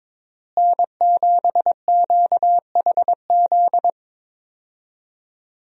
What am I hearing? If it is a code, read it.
Morse code, heard as a keyed tone, about 22 wpm, 700 Hz. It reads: N7QHZ
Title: USA Call Signs - 5 Characters - Repeated 2x - 22wpm